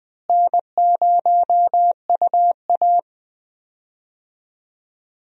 Morse code, heard as a keyed tone, about 20 wpm, 700 Hz. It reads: N0UA
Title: USA Call Signs - 4 Characters - Repeated 3x - 20wpm